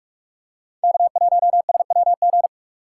Morse code, heard as a keyed tone, about 44 wpm, 690 Hz. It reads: K1SWG
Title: USA Call Signs - 5 Characters - 6x Character Spacing - 45wpm